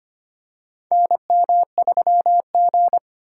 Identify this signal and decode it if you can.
Morse code, heard as a keyed tone, about 25 wpm, 700 Hz. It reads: NM3G
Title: USA Call Signs - 4 Characters - 2x Character Spacing - 25wpm